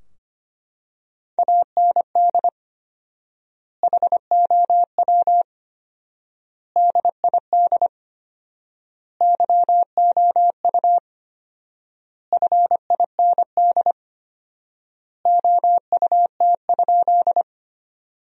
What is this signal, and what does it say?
Morse code, heard as a keyed tone, about 25 wpm, 700 Hz. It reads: AND HOW DID YOU FIND OUT?
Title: 4x Word-Spacing - Sentences from Top 100 Words - 25wpm